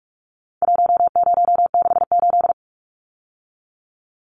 Morse code, heard as a keyed tone, about 44 wpm, 700 Hz. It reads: 1068
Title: Numbers - 4-Digit 45wpm